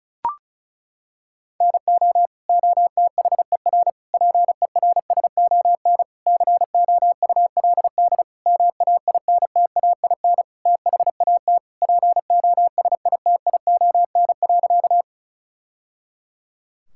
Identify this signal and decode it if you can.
Morse code, heard as a keyed tone, about 36 wpm, 700 Hz. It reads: NO OTHER PERSON COULD MAINTAIN THAT POSITION.
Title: Sentences from Top 1000 Words - Review Part 5 - 35wpm